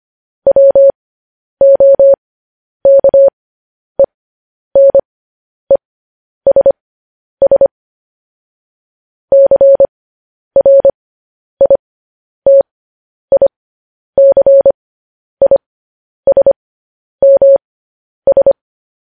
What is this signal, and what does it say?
Morse code, heard as a keyed tone, about 25 wpm, spaced as for 10 wpm, 550 Hz. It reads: WOKENESS CRITICISMS
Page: news headlines in morse code at 10 wpm